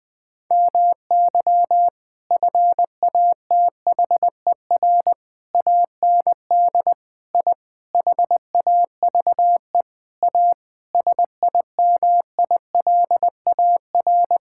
Morse code, heard as a keyed tone, about 21 wpm, 700 Hz. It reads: MY FATHER AND I HAVE A SIMILAR
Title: Sentences from Top 1000 Words - Review Part 6 - 20wpm